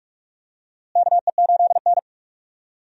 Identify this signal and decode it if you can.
Morse code, heard as a keyed tone, about 42 wpm, 700 Hz. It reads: KE8N
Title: USA Call Signs - 4 Characters - Repeated 3x - 45wpm